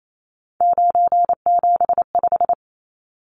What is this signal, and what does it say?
Morse code, heard as a keyed tone, about 28 wpm, 700 Hz. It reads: 975
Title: Numbers - 3-Digit 28wpm